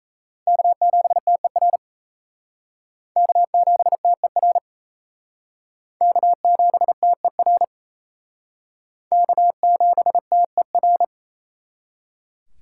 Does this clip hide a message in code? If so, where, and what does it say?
Yes: Morse code from 0.5 to 11.0 s, heard as a keyed tone, about 36 wpm, 700 Hz. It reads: K7TER K7TER K7TER K7TER